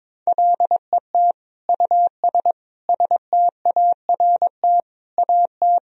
Morse code, heard as a keyed tone, about 21 wpm, 700 Hz. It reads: LET US START AT
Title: Sentences from Top 1000 Words - Review Part 1 - 22wpm